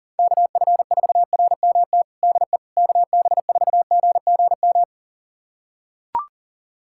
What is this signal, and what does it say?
Morse code, heard as a keyed tone, about 37 wpm, 700 Hz. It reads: KF4RMT DE KB4GGM
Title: USA Callsign DE Callsign - 6 Characters - 40wpm